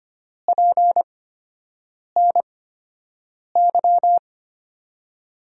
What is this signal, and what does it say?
Morse code, heard as a keyed tone, about 25 wpm, spaced as for 7 wpm, 700 Hz. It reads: PNY
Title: USA Call Signs - 6 Characters - 8x Character Spacing - 25wpm